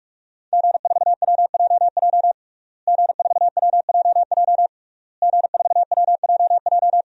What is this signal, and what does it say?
Morse code, heard as a keyed tone, about 46 wpm, 700 Hz. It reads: G4WJJ G4WJJ G4WJJ
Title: DXCC-100 Call Signs - 5 Characters - Repeated 3x - 45wpm